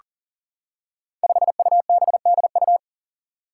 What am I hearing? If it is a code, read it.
Morse code, heard as a keyed tone, about 40 wpm, 700 Hz. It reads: 5UBDU
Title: Five Challenging Letter-Number Combinations 40wpm